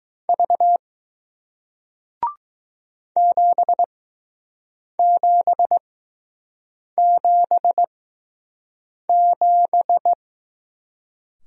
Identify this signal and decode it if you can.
Morse code, heard as a keyed tone, about 20 wpm, spaced as for 6 wpm, 700 Hz. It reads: V 7777